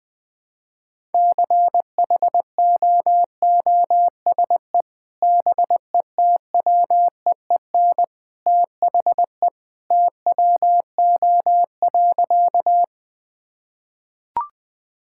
Morse code, heard as a keyed tone, about 20 wpm, 700 Hz. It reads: CHOOSE BETWEEN THE TWO.
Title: Sentences with 2-Word Phrases - 20wpm